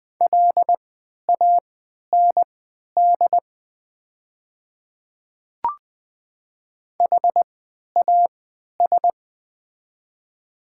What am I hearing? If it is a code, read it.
Morse code, heard as a keyed tone, about 20 wpm, spaced as for 11 wpm, 700 Hz. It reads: LAND HAS
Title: 3x Character Spacing - Top 300 Words - 20wpm